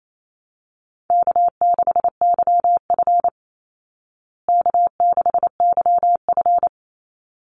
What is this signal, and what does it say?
Morse code, heard as a keyed tone, about 28 wpm, 700 Hz. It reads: K6YF K6YF
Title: Easy Call Signs Repeated 2x - 28wpm